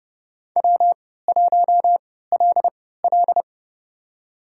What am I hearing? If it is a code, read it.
Morse code, heard as a keyed tone, about 30 wpm, spaced as for 17 wpm, 700 Hz. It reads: W1LL
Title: USA Call Signs - 4 Characters - 3x Character Spacing - 30wpm